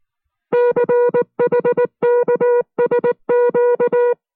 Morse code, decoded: CHKSQ